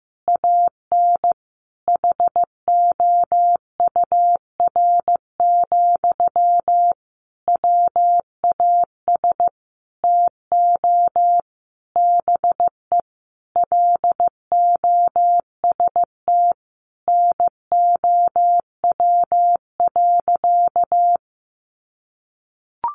Morse code, heard as a keyed tone, about 15 wpm, 700 Hz. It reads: AN HOUR, WAS TO BE LOST NOW.
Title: Sentences from Top 1400 Words 15wpm